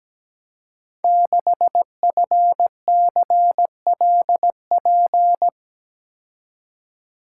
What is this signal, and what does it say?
Morse code, heard as a keyed tone, about 17 wpm, 700 Hz. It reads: 6FCLP
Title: Code Groups - 5 Letter:Numbers - 17wpm